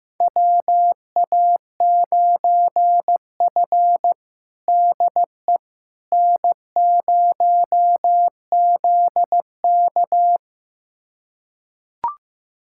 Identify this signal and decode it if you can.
Morse code, heard as a keyed tone, about 15 wpm, 700 Hz. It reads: WA9F DE N0ZK
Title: USA Callsign DE Callsign - 4 Characters - 15wpm